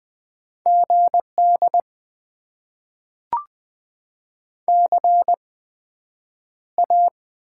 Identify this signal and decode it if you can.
Morse code, heard as a keyed tone, about 20 wpm, spaced as for 9 wpm, 700 Hz. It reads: GD CA
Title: Top 90 QSO Elements - 8x Character Spacing - 20wpm